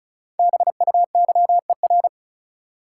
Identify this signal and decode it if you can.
Morse code, heard as a keyed tone, about 36 wpm, 700 Hz. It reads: BUYER
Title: Five Letter Words - 35wpm